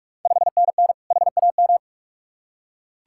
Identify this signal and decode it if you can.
Morse code, heard as a keyed tone, about 44 wpm, 690 Hz. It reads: 5NN HAM
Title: Ontario QSO Party Exchange 45wpm